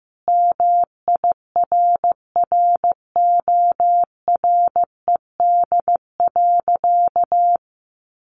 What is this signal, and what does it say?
Morse code, heard as a keyed tone, about 15 wpm, 700 Hz. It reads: MIRRORED.
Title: A Sound Of Thunder 15wpm